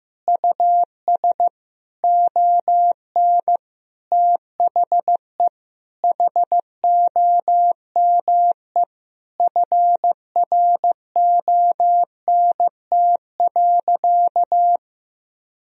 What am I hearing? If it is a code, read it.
Morse code, heard as a keyed tone, about 15 wpm, 700 Hz. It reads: US ON THE HOME FRONT.
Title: Sentences from Top 1000 Words - Review Part 2 - 15wpm